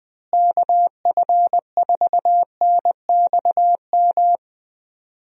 Morse code, heard as a keyed tone, about 20 wpm, 700 Hz. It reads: KF4NXM
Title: USA Call Signs - 6 Characters - Repeated 2x - 20wpm